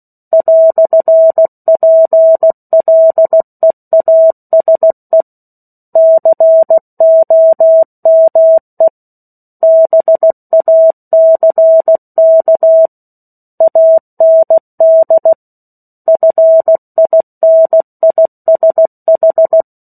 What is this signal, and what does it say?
Morse code, heard as a keyed tone, about 16 wpm, 650 Hz. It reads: "PLEASE COME BACK AND FINISH
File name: 03-Chapter-03 16 WPM